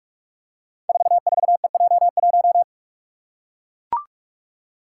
Morse code, heard as a keyed tone, about 44 wpm, 690 Hz. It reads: 44EJ1
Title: Code Groups - 5 Letter:Numbers - 45wpm